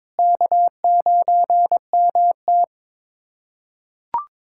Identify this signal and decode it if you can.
Morse code, heard as a keyed tone, about 22 wpm, 700 Hz. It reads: K9MT
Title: USA Callsign DE Callsign - 4 Characters - 22wpm